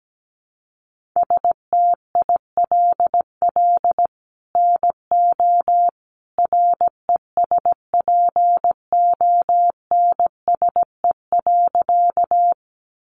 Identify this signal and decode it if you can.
Morse code, heard as a keyed tone, about 17 wpm, 700 Hz. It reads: STILL NO RESPONSE.
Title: Sentences from Top 1400 Words 17wpm